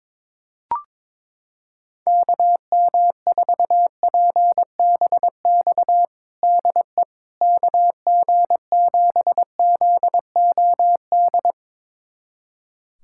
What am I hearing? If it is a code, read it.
Morse code, heard as a keyed tone, about 21 wpm, 700 Hz. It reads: KM4PBX DE KG7ZOD
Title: USA Callsign DE Callsign - 6 Characters - 22wpm